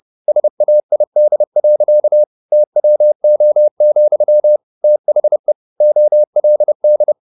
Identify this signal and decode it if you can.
Morse code, heard as a keyed tone, about 30 wpm, 600 Hz. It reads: SAID. TWO, THE OLD
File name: the old man and the sea / the old man and the sea 1 30wpm